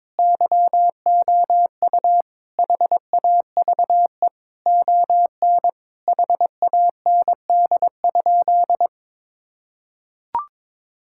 Morse code, heard as a keyed tone, about 22 wpm, 700 Hz. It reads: YOU HAVE ON HAND?
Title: Sentences from Top 1000 Words - Review Part 5 - 22wpm